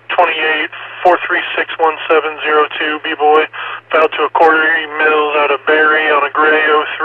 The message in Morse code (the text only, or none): none